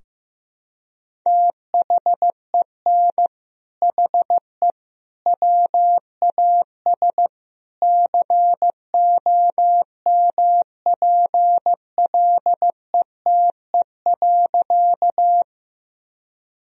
Morse code, heard as a keyed tone, about 15 wpm, 700 Hz. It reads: THEN HE WAS COMPLETE.
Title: Sentences from Top 1000 Words - Review Part 2 - 15wpm